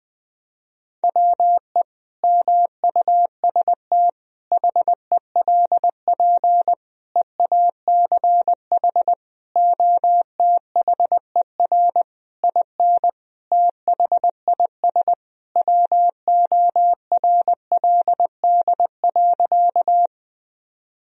Morse code, heard as a keyed tone, about 21 wpm, 700 Hz. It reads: WE MUST HELP EACH OTHER IN THIS WORLD.
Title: Sentences from Top 1000 Words - Review Part 1 - 20wpm